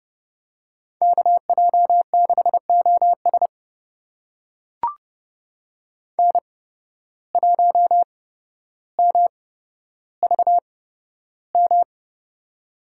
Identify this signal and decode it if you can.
Morse code, heard as a keyed tone, about 31 wpm, spaced as for 13 wpm, 700 Hz. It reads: KJ6OS N1MVM
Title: USA Call Signs - 5 Characters - 8x Character Spacing - 30wpm